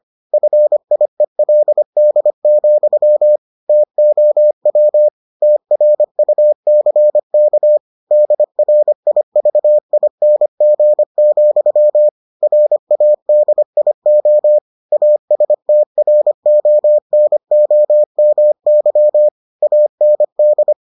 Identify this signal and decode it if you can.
Morse code, heard as a keyed tone, about 25 wpm, 600 Hz. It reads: FIELD, TOW TRUCK DRIVING, RADIO ASTRONOMY AND